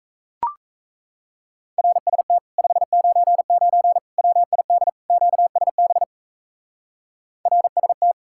Morse code, heard as a keyed tone, about 34 wpm, 700 Hz. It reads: RST 599 WID QSB RST
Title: Speed-Racing Common QSO Phrases - 28wpm